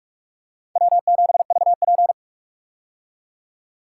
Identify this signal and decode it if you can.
Morse code, heard as a keyed tone, about 46 wpm, 700 Hz. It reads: W7VP
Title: K1USN SST QSO Practice - 45wpm